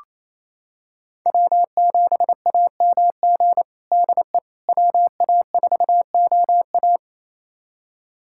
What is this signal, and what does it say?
Morse code, heard as a keyed tone, about 27 wpm, 700 Hz. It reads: W7AMG DE WA4OA
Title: USA Callsign DE Callsign - 5 Characters - 28wpm